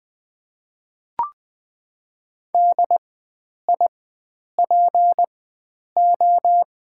Morse code, heard as a keyed tone, about 20 wpm, spaced as for 9 wpm, 700 Hz. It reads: DIPO
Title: Top 90 QSO Elements - 4x Character Spacing - 20wpm